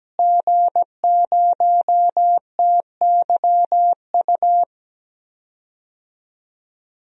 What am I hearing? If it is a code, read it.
Morse code, heard as a keyed tone, about 17 wpm, 700 Hz. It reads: G0TYU
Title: DXCC-100 Call Signs - 5 Characters - Repeated 2x - 17wpm